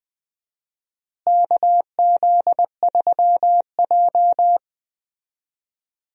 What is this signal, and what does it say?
Morse code, heard as a keyed tone, about 20 wpm, 700 Hz. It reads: KZ3J
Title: USA Call Signs - 4 Characters - Repeated 3x - 20wpm